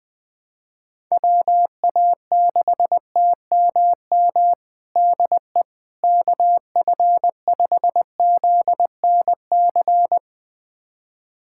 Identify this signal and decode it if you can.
Morse code, heard as a keyed tone, about 20 wpm, 700 Hz. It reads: WA6TMM DE KF5ZNC
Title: USA Callsign DE Callsign - 6 Characters - 20wpm